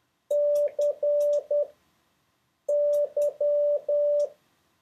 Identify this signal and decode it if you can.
Morse code, heard as a keyed tone, about 10 wpm, spaced as for 6 wpm, 580 Hz. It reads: CY
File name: 1of6-letters-D-H-U-C-Y